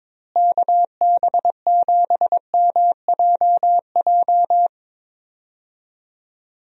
Morse code, heard as a keyed tone, about 21 wpm, 700 Hz. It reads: KB7MJJ
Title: USA Call Signs - 6 Characters - Repeated 3x - 22wpm